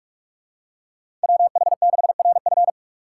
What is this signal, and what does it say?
Morse code, heard as a keyed tone, about 45 wpm, 700 Hz. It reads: WH6RF